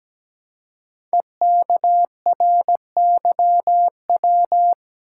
Morse code, heard as a keyed tone, about 17 wpm, 700 Hz. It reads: EKRYW
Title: Code Groups - 5 Letters - 17wpm